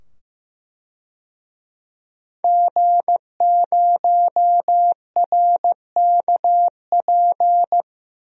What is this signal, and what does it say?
Morse code, heard as a keyed tone, about 15 wpm, 700 Hz. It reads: G0RKP
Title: DXCC-100 Call Signs - 5 Characters - 15wpm